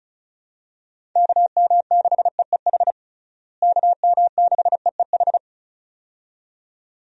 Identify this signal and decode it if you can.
Morse code, heard as a keyed tone, about 34 wpm, 700 Hz. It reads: KM6EEH KM6EEH
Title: USA Call Signs - 6 Characters - Repeated 2x - 35wpm